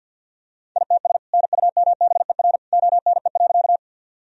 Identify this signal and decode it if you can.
Morse code, heard as a keyed tone, about 49 wpm, 700 Hz. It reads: ITS NUMBER ONE.
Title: Sentences from Top 1000 Words - Review Part 1 - 50wpm